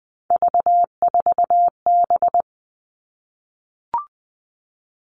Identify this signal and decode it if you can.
Morse code, heard as a keyed tone, about 20 wpm, 700 Hz. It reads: V4B
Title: All Cut Numbers - 3-Digit 20wpm